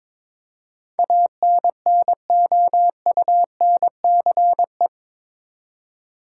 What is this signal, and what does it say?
Morse code, heard as a keyed tone, about 22 wpm, 700 Hz. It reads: ANNOUNCE
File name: practice / 22wpm / Eight Letter Words - 22wpm